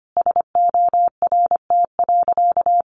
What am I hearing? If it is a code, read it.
Morse code, heard as a keyed tone, about 25 wpm, 700 Hz. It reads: SORT.
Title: Sentences from Top 1300 Words 25wpm